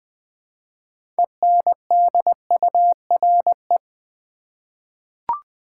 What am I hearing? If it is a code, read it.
Morse code, heard as a keyed tone, about 20 wpm, 700 Hz. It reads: ENDURE
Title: 6 Letter Words - First 3 Letters Given - 20wpm